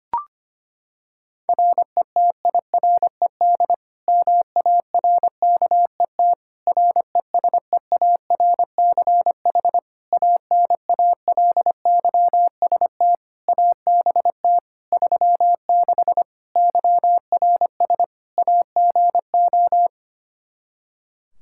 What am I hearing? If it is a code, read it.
Morse code, heard as a keyed tone, about 25 wpm, 700 Hz. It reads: RETIRED MARKET RESEARCH ANALYST ABT 36 YRS AGO
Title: Practice QSO - Exchange 3 - More Info - 25wpm